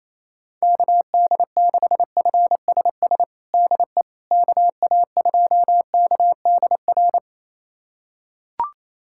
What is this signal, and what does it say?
Morse code, heard as a keyed tone, about 28 wpm, 700 Hz. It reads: KD6FSS DE KA2KDR
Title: USA Callsign DE Callsign 28wpm